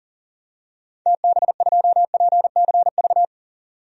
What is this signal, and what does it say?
Morse code, heard as a keyed tone, about 40 wpm, 700 Hz. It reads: TB2PCV
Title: DXCC-100 Call Signs - 6 Characters - 40wpm